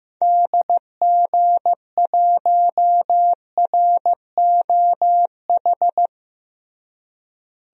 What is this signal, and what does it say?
Morse code, heard as a keyed tone, about 15 wpm, 700 Hz. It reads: DG1ROH
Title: DXCC-100 Call Signs - 6 Characters - Repeated 2x - 15wpm